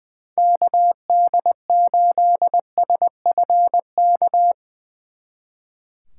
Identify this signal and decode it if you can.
Morse code, heard as a keyed tone, about 20 wpm, 700 Hz. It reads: KD8SFK
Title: Speed Racing - USA 2x3 Call Signs 20wpm